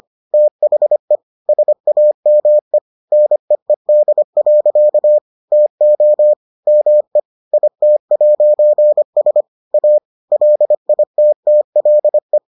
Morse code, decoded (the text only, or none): THE SAME NEED. TO ME IT'S A LITTLE